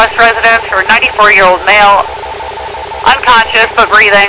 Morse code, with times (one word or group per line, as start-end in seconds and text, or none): none